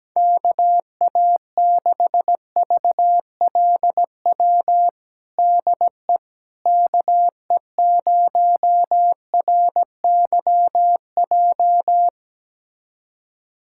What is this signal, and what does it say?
Morse code, heard as a keyed tone, about 17 wpm, 700 Hz. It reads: KA6VLW DE KE0RYJ
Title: USA Callsign DE Callsign - 6 Characters - 17wpm